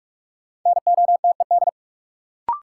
Morse code, heard as a keyed tone, about 42 wpm, 700 Hz. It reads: NOTED
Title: Sets of 5 Words - New Word Drill - 45wpm